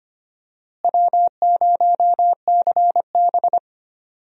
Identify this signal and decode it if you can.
Morse code, heard as a keyed tone, about 25 wpm, 700 Hz. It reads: W0CB
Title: USA Call Signs - 4 Characters - Repeated 3x - 25wpm